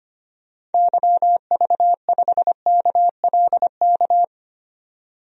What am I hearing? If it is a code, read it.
Morse code, heard as a keyed tone, about 25 wpm, 700 Hz. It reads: YV5KLK
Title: DXCC-100 Call Signs - 6 Characters - 25wpm